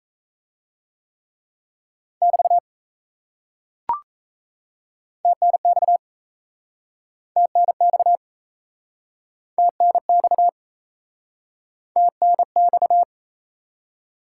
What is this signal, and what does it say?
Morse code, heard as a keyed tone, about 40 wpm, 700 Hz. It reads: <BT> TNX TNX TNX TNX